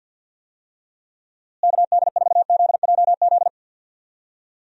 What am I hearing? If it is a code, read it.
Morse code, heard as a keyed tone, about 47 wpm, 700 Hz. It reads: KD4ZJZ
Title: USA Call Signs - 6 Characters - Repeated 2x - 50wpm